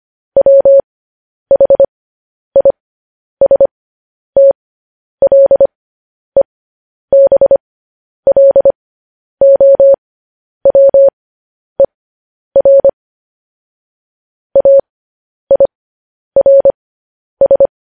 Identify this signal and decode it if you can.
Morse code, heard as a keyed tone, about 25 wpm, spaced as for 10 wpm, 550 Hz. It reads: WHISTLEBLOWER AIRS
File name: podcast / News headlines-1x-25x10